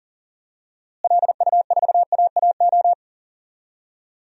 Morse code, heard as a keyed tone, about 40 wpm, 700 Hz. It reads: LU4AAO
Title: DXCC-100 Call Signs - 6 Characters - Repeated 3x - 40wpm